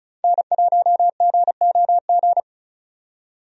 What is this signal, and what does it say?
Morse code, heard as a keyed tone, about 36 wpm, 700 Hz. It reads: N1GOG
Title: USA Call Signs - 5 Characters - 35wpm